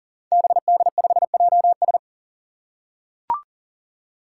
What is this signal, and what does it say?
Morse code, heard as a keyed tone, about 37 wpm, 700 Hz. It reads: BD5JS
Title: Five Challenging Letter-Number Combinations 40wpm